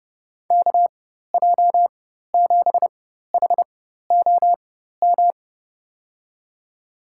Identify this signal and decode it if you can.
Morse code, heard as a keyed tone, about 30 wpm, spaced as for 14 wpm, 700 Hz. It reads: KJ7HOM
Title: USA Call Signs - 4x Character Spacing - 30wpm